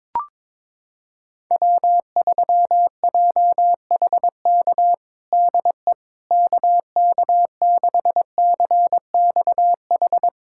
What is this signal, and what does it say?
Morse code, heard as a keyed tone, about 22 wpm, 700 Hz. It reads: W3JHK DE KK6CXH
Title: USA Callsign DE Callsign 22wpm